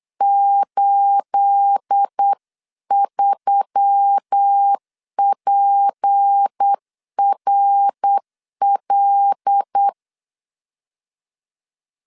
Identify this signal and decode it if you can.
Morse code, heard as a keyed tone, about 8 wpm, 800 Hz. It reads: 83PRL
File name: 8wpm